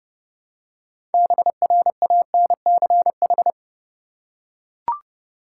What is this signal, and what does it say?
Morse code, heard as a keyed tone, about 30 wpm, 700 Hz. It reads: BRANCH
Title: Six Letter Words - 30wpm